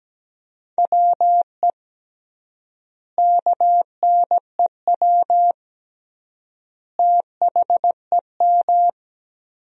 Morse code, heard as a keyed tone, about 17 wpm, 700 Hz. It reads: WE KNEW THEM
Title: 3x Word-Spacing - Sentences from Top 400 Words - 17wpm